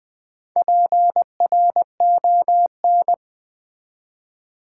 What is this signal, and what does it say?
Morse code, heard as a keyed tone, about 20 wpm, 700 Hz. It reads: PRON
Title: Build A Word - 4 Letters At A Time - 10-Letter Words - 20wpm